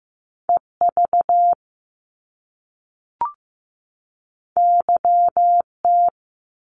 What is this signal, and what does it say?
Morse code, heard as a keyed tone, about 15 wpm, 700 Hz. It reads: EV YT
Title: Top Two Letter Combinations in Words 15wpm